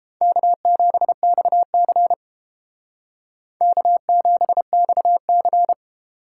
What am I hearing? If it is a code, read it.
Morse code, heard as a keyed tone, about 31 wpm, 700 Hz. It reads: K7XC K7XC